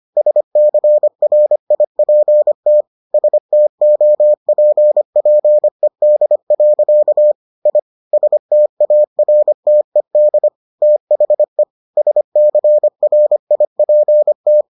Morse code, heard as a keyed tone, about 25 wpm, 600 Hz. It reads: SCRIPT STOPPED. I STARTED THE SCRIPT